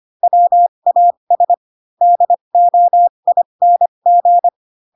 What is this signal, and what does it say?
Morse code, heard as a keyed tone, about 25 wpm, spaced as for 22 wpm, 700 Hz. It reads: WAS DOING
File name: Pinocchio / Pinocchio ch3 18 25 ik1hge